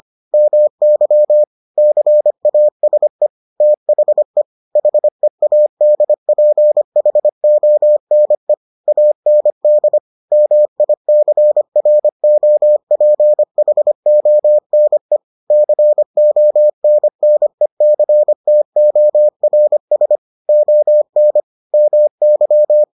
Morse code, heard as a keyed tone, about 25 wpm, 600 Hz. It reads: MY CASE THE HEADPHONE AND MICROPHONE CONNECTORS ON MY